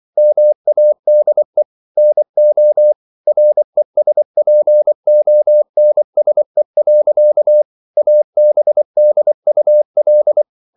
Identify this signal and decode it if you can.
Morse code, heard as a keyed tone, about 24 wpm, 600 Hz. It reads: MADE NO RESPONSE. ABDUL